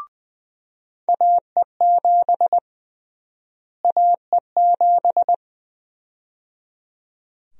Morse code, heard as a keyed tone, about 20 wpm, 700 Hz. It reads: AE7 AE7
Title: USA Call Sign Prefixes - Repeated 2x - 20wpm